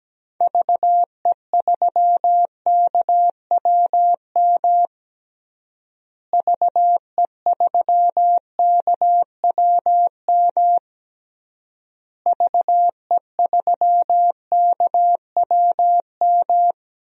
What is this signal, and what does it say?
Morse code, heard as a keyed tone, about 17 wpm, 700 Hz. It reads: VE3KWM VE3KWM VE3KWM
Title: DXCC-100 Call Signs - 6 Characters - Repeated 3x - 17wpm